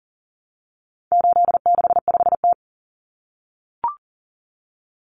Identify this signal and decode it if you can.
Morse code, heard as a keyed tone, about 40 wpm, 700 Hz. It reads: 865T
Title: Common Cut Numbers - 4-Digit 40wpm